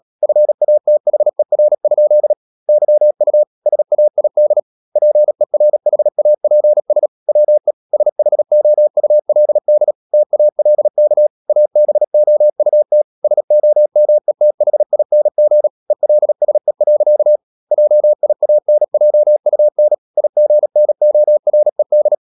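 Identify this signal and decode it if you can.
Morse code, heard as a keyed tone, about 37 wpm, 600 Hz. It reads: FATHER? YU SAID PERHAPS WE SHOULD TALK ABOUT SOMETHING ELSE. JIANJUN IGNORED
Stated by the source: Morse code practice at 35–39 wpm sending English text of Ken Follett's novel Never, from CW med Gnister.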